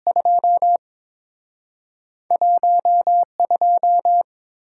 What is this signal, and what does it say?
Morse code, heard as a keyed tone, about 30 wpm, 700 Hz. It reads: 2 1 2